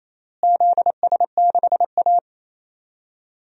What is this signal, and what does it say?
Morse code, heard as a keyed tone, about 28 wpm, 700 Hz. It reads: ZS6A
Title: DXCC-100 Call Signs - 4 Characters - Repeated 3x - 28wpm